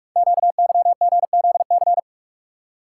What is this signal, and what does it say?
Morse code, heard as a keyed tone, about 44 wpm, 700 Hz. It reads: QYGZC
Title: Code Groups - 5 Letters - 45wpm